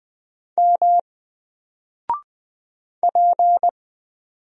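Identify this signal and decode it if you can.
Morse code, heard as a keyed tone, about 20 wpm, 700 Hz. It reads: M P
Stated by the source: Single letter and number Morse drill at 20 wpm, from N4LCD.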